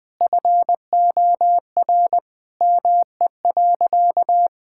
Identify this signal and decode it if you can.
Morse code, heard as a keyed tone, about 21 wpm, 700 Hz. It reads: FOR ME.
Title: Sentences from Top 1000 Words - Review Part 6 - 20wpm